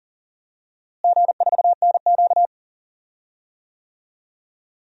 Morse code, heard as a keyed tone, about 40 wpm, 700 Hz. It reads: G4NQ
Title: DXCC-100 Call Signs - 4 Characters - Repeated 2x - 40wpm